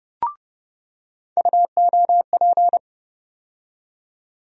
Morse code, heard as a keyed tone, about 29 wpm, 700 Hz. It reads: UOP
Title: USA Call Sign Postfixes - 3 Letters 30wpm